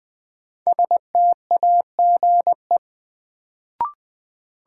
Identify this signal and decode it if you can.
Morse code, heard as a keyed tone, about 21 wpm, 700 Hz. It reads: STAGE
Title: Sentences from Top 1200 Words - New Word Drill - 20wpm